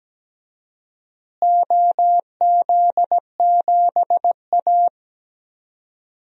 Morse code, heard as a keyed tone, about 17 wpm, 700 Hz. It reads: OZ7A
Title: DXCC-100 Call Signs - 4 Characters - Repeated 3x - 17wpm